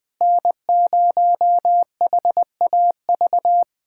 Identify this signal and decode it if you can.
Morse code, heard as a keyed tone, about 20 wpm, 700 Hz. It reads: N0HAV
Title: USA Callsign DE Callsign - 5 Characters - 20wpm